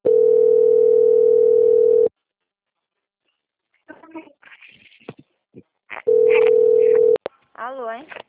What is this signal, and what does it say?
Ringback tone (North America), 440 + 480 Hz, with speech.
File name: out- - - - - .25154